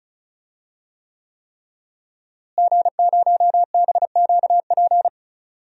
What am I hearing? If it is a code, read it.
Morse code, heard as a keyed tone, about 35 wpm, 700 Hz. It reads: G0BQP
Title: DXCC-100 Call Signs - 5 Characters - Repeated 3x - 35wpm